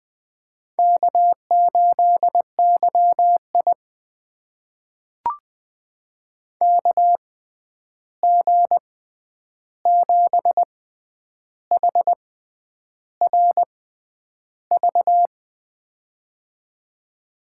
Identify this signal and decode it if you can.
Morse code, heard as a keyed tone, about 20 wpm, spaced as for 9 wpm, 700 Hz. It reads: K8YI KG7HRV
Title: USA Call Signs - 6x Character Spacing - 20wpm